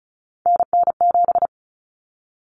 Morse code, heard as a keyed tone, about 34 wpm, 700 Hz. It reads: NN7
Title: Common Cut Numbers - 3-Digit 35wpm